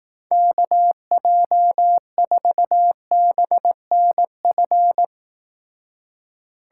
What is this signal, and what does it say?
Morse code, heard as a keyed tone, about 18 wpm, 700 Hz. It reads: KJ4BNF
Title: Speed Racing - USA 2x3 Call Signs 15wpm